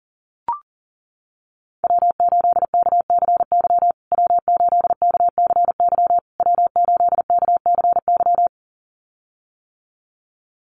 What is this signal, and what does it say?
Morse code, heard as a keyed tone, about 40 wpm, 700 Hz. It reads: W8KCY W8KCY W8KCY
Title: USA Call Signs Repeated 3x 40wpm